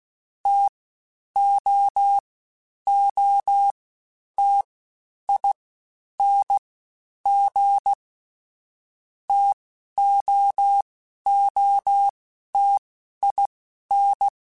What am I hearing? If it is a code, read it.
Morse code, heard as a keyed tone, about 16 wpm, 800 Hz. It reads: T O O T I N G T O O T I N
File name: Difficult Combos / Difficult Combos-mp3s / T AOEI @15